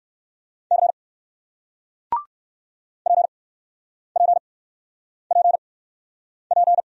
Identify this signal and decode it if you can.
Morse code, heard as a keyed tone, about 58 wpm, spaced as for 9 wpm, 700 Hz. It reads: C PPPP